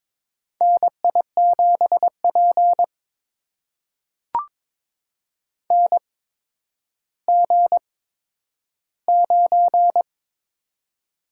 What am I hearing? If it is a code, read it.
Morse code, heard as a keyed tone, about 22 wpm, spaced as for 11 wpm, 700 Hz. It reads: NI7P NG9
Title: USA Call Signs - 4 Characters - 8x Character Spacing - 22wpm